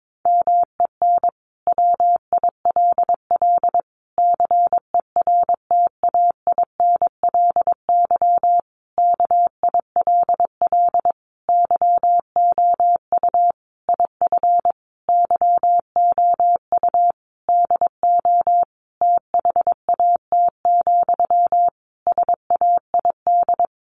Morse code, heard as a keyed tone, about 22 wpm, 700 Hz. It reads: MEN WILL CERTAINLY KILL YOU IF YOU DO THAT, SAID
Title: Sentences from Top 1500 Words 22wpm